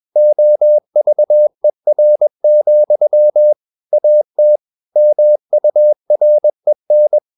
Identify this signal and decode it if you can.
Morse code, heard as a keyed tone, about 21 wpm, 600 Hz. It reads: OVER, AT MUREN
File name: Ferie QRS 21wpm